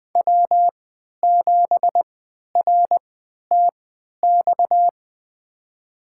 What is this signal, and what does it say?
Morse code, heard as a keyed tone, about 20 wpm, spaced as for 11 wpm, 700 Hz. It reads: W7RTX
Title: USA Call Signs - 5 Characters - 3x Character Spacing - 20wpm